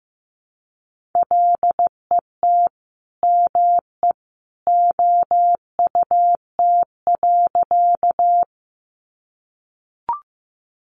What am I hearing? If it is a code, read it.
Morse code, heard as a keyed tone, about 15 wpm, 700 Hz. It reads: LET ME OUT.
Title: Sentences from Top 300 Words 15wpm